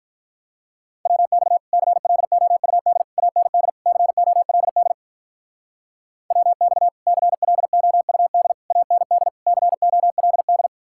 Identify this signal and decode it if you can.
Morse code, heard as a keyed tone, about 43 wpm, 700 Hz. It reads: WX CLOUD AND COLD WX CLOUD AND COLD
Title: Speed-Racing Common QSO Phrases - 35wpm